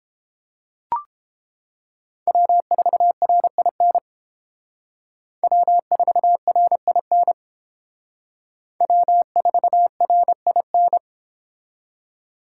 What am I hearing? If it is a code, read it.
Morse code, heard as a keyed tone, about 31 wpm, 700 Hz. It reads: W4RIN W4RIN W4RIN